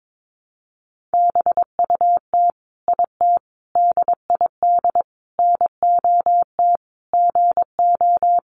Morse code, heard as a keyed tone, about 22 wpm, 700 Hz. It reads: BUT IT DID NOT GO
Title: Sentences from Top 600 Words 22wpm